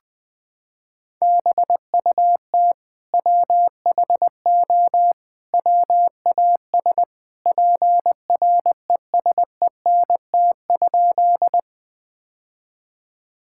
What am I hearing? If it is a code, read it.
Morse code, heard as a keyed tone, about 20 wpm, 700 Hz. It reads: BUT WHO WAS PRESENT?
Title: Sentences from Top 1000 Words - Review Part 5 - 20wpm